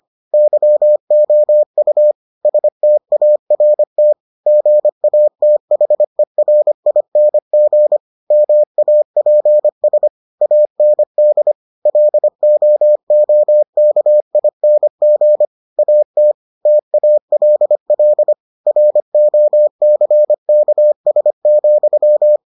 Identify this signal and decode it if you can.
Morse code, heard as a keyed tone, about 25 wpm, 600 Hz. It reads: YOU START GATHERING MAPS AND LOOKING AT TALL ROCKS,